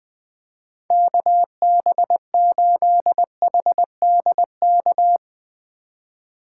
Morse code, heard as a keyed tone, about 20 wpm, 700 Hz. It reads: KB8HDK
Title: USA Call Signs - 6 Characters - 3x Character Spacing - 20wpm